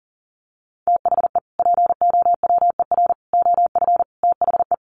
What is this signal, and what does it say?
Morse code, heard as a keyed tone, about 40 wpm, 700 Hz. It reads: THE POWER OF THE
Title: Sets of 4 Words 40wpm